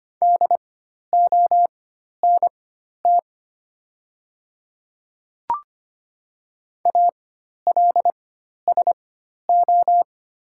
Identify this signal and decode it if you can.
Morse code, heard as a keyed tone, about 25 wpm, spaced as for 12 wpm, 700 Hz. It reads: DONT ALSO
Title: 4x Character Spacing - Top 500 Words - 25wpm